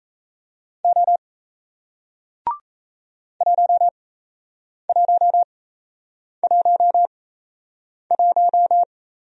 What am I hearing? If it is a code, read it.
Morse code, heard as a keyed tone, about 37 wpm, spaced as for 8 wpm, 700 Hz. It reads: O 1111